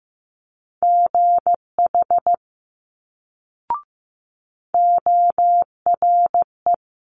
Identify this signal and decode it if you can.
Morse code, heard as a keyed tone, about 15 wpm, 700 Hz. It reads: GH ORE
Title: Multi-letter Phonemes Quick Fire 15wpm